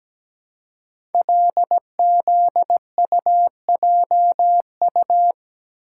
Morse code, heard as a keyed tone, about 17 wpm, 700 Hz. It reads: LZUJU
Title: Code Groups - 5 Letters - 17wpm